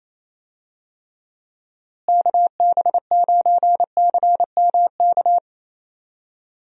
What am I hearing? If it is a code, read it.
Morse code, heard as a keyed tone, about 28 wpm, 700 Hz. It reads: KB9CMK
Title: USA Call Signs - 6 Characters - Repeated 2x - 28wpm